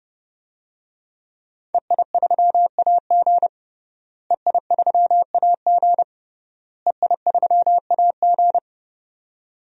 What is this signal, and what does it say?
Morse code, heard as a keyed tone, about 30 wpm, 700 Hz. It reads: EI3AG EI3AG EI3AG
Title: DXCC-100 Call Signs - 5 Characters - Repeated 3x - 30wpm